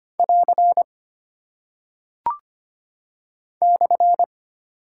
Morse code, heard as a keyed tone, about 25 wpm, 700 Hz. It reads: <AR> /
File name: Single Punctuation-Prosign - 25wpm